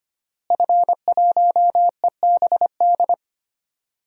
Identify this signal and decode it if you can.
Morse code, heard as a keyed tone, about 25 wpm, 700 Hz. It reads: F1EBD
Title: DXCC-100 Call Signs - 5 Characters - Repeated 3x - 25wpm